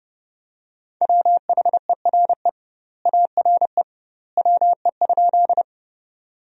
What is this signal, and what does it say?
Morse code, heard as a keyed tone, about 30 wpm, 700 Hz. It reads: WHERE ARE WE?
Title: 2x Word-Spacing - Sentences from Top 200 Words - 30wpm